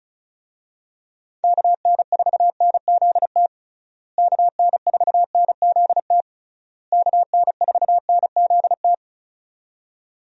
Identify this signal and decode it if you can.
Morse code, heard as a keyed tone, about 34 wpm, 700 Hz. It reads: KN4NZT KN4NZT KN4NZT
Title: USA Call Signs - 6 Characters - Repeated 3x - 35wpm